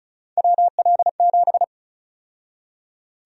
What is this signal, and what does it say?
Morse code, heard as a keyed tone, about 36 wpm, 700 Hz. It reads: WL7
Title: USA Call Sign Prefixes - 35wpm